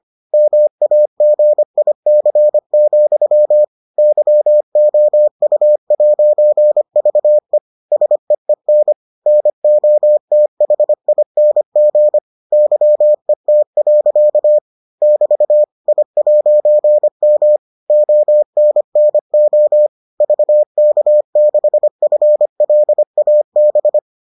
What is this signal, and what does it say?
Morse code, heard as a keyed tone, about 25 wpm, 600 Hz. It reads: MAGIC, YOU'VE SEEN NOTHING YET. <BT> I'M ONNO VK6FLAB